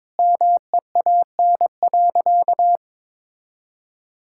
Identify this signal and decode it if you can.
Morse code, heard as a keyed tone, about 22 wpm, 700 Hz. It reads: MEAN.
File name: practice / 22wpm / Sentences with 2-Word Phrases - 22wpm